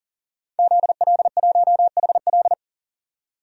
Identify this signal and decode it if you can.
Morse code, heard as a keyed tone, about 40 wpm, 700 Hz. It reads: ZL1HL
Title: DXCC-100 Call Signs - 5 Characters - 40wpm